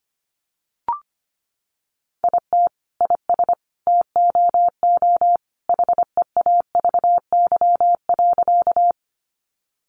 Morse code, heard as a keyed tone, about 25 wpm, 700 Hz. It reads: IT IS TOO HEAVY.
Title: Sentences from Top 1100 Words 25wpm